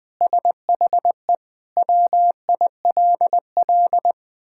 Morse code, heard as a keyed tone, about 21 wpm, 700 Hz. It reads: SHE WILL
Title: Sentences with 7-Word Phrases - 20wpm